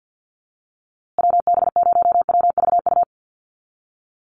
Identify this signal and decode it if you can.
Morse code, heard as a keyed tone, about 50 wpm, 700 Hz. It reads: WB0WVU
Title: USA Call Signs Repeated 2x 50wpm